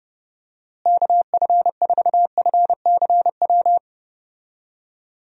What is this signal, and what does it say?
Morse code, heard as a keyed tone, about 30 wpm, 700 Hz. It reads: KF4FCW
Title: USA Call Signs - 6 Characters - Repeated 3x - 30wpm